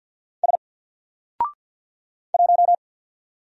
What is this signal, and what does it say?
Morse code, heard as a keyed tone, about 48 wpm, 700 Hz. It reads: S 1
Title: Single Letter-Number Alt 50wpm